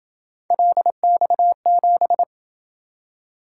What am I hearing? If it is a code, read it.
Morse code, heard as a keyed tone, about 26 wpm, 700 Hz. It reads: LX7